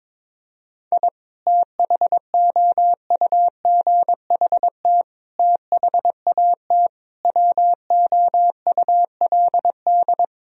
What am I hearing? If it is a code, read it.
Morse code, heard as a keyed tone, about 22 wpm, 700 Hz. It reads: I THOUGHT THAT WOULD
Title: Sentences from Top 1000 Words - Review Part 4 - 22wpm